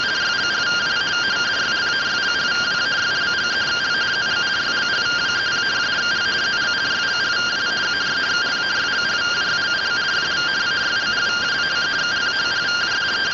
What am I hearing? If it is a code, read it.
Two-tone teleprinter signal, about 170 Hz shift.